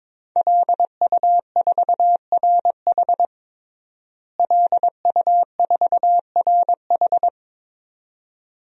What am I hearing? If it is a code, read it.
Morse code, heard as a keyed tone, about 22 wpm, 700 Hz. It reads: LU4RH LU4RH
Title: DXCC-100 Call Signs - 5 Characters - Repeated 2x - 22wpm